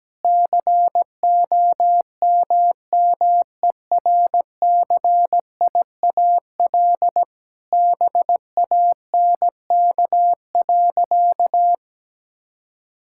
Morse code, heard as a keyed tone, about 17 wpm, 700 Hz. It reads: COMMERCIAL BANK.
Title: Sentences from Top 1000 Words - Review Part 6 - 17wpm